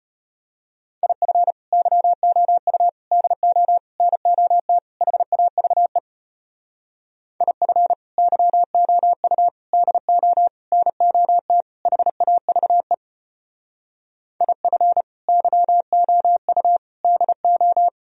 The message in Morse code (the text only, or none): IF YOU DO NOT HAVE IF YOU DO NOT HAVE IF YOU DO